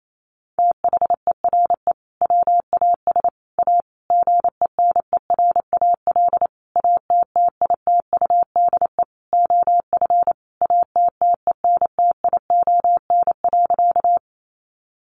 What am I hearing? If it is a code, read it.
Morse code, heard as a keyed tone, about 28 wpm, 700 Hz. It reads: THERE WAS A GENERAL ATTITUDE OF ATTENTION.
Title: Sentences from Top 1300 Words 28wpm